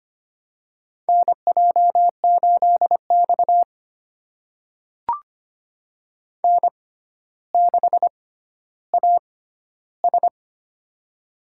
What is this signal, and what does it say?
Morse code, heard as a keyed tone, about 25 wpm, spaced as for 13 wpm, 700 Hz. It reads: NJ8X N6AS
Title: USA Call Signs - 4 Characters - 6x Character Spacing - 25wpm